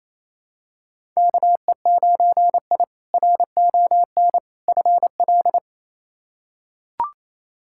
Morse code, heard as a keyed tone, about 28 wpm, 700 Hz. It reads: KE9I RON FL
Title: K1USN SST QSO Practice - 28wpm